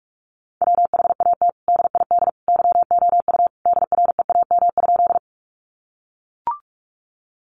Morse code, heard as a keyed tone, about 44 wpm, 700 Hz. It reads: WHAT DID YOU DREAM?
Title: Sentences from Top 1500 Words 45wpm